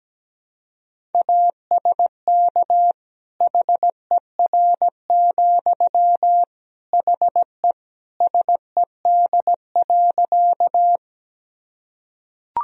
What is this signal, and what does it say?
Morse code, heard as a keyed tone, about 17 wpm, 700 Hz. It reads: ASK HER, HE SED.
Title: Sentences from Top 200 Words and Abbreviations 17wpm